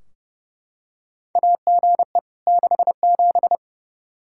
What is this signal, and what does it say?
Morse code, heard as a keyed tone, about 30 wpm, 700 Hz. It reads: AGE 67
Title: K1USN SST QSO Practice 30wpm